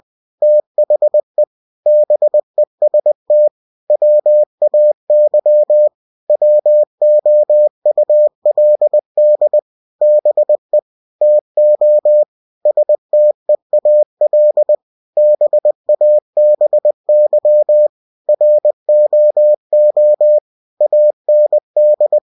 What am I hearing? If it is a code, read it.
Morse code, heard as a keyed tone, about 20 wpm, 600 Hz. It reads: THE BEST WAY WOULD BE TO STEAL BABY ROO AND
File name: cw-books / winnie-the-pooh / winnie-20wpm-page-0061